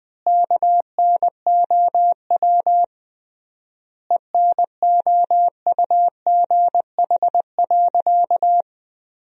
Morse code, heard as a keyed tone, about 20 wpm, 700 Hz. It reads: KNOW ENOUGH.
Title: 3x Word-Spacing - Sentences from Top 400 Words - 20wpm